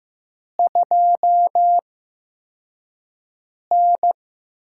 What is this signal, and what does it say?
Morse code, heard as a keyed tone, about 15 wpm, spaced as for 4 wpm, 700 Hz. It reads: 2N
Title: USA Call Signs - 4 Characters - 8x Character Spacing - 15wpm